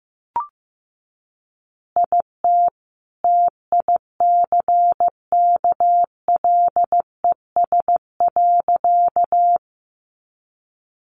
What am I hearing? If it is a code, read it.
Morse code, heard as a keyed tone, about 15 wpm, 700 Hz. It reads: IT TICKLES.
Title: Additional Easy Phrases 15wpm